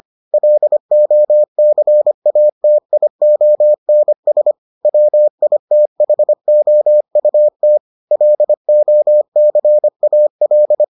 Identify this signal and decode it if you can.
Morse code, heard as a keyed tone, about 25 wpm, 600 Hz. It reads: LOCATIONS WITHOUT LOCAL